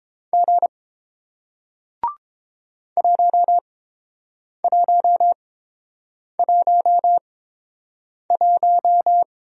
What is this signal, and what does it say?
Morse code, heard as a keyed tone, about 32 wpm, spaced as for 7 wpm, 700 Hz. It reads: G 1111